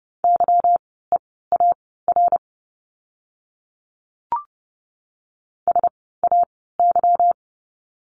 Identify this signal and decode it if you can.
Morse code, heard as a keyed tone, about 30 wpm, spaced as for 17 wpm, 700 Hz. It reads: YEAR SAY
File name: course / 30:10wpm / 237 Lesson 38f - Letter q - Words 30wpm